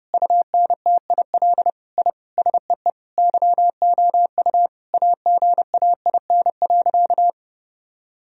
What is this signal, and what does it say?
Morse code, heard as a keyed tone, about 30 wpm, 700 Hz. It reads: UNTIL I SEE YOU AGAIN.
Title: Sentences from Top 1000 Words - Review Part 1 - 30wpm